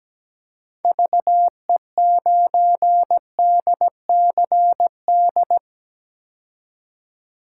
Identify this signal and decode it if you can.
Morse code, heard as a keyed tone, about 17 wpm, 700 Hz. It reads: VE9DCD
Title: K1USN SST QSO Practice - 17wpm